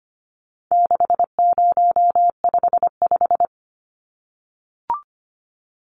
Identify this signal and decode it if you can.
Morse code, heard as a keyed tone, about 25 wpm, 700 Hz. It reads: 6055
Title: Numbers - 4-Digit 25wpm